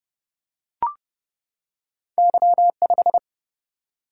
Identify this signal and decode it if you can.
Morse code, heard as a keyed tone, about 30 wpm, 700 Hz. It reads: Y5
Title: Single-Letter Single-Number 30wpm